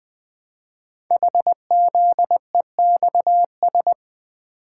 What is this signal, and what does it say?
Morse code, heard as a keyed tone, about 20 wpm, 700 Hz. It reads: HZEXS
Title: Code Groups - 5 Letters - 20wpm